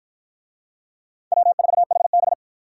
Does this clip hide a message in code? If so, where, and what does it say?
Yes: Morse code from 1.3 to 2.3 s, heard as a keyed tone, about 49 wpm, 700 Hz. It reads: W4HB